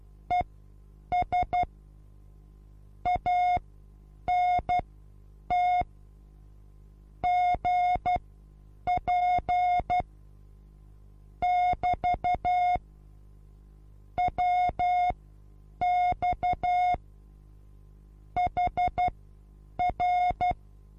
Morse code, heard as a keyed tone, about 12 wpm, spaced as for 8 wpm, 720 Hz. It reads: ES ANT GP <BT> WX HR
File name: Onlinelehrgang / morsen / DJ4UF Morsen23